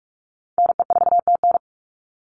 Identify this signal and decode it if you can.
Morse code, heard as a keyed tone, about 46 wpm, 690 Hz. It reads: NE4TN
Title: NA SOTA Chaser Callsigns 45wpm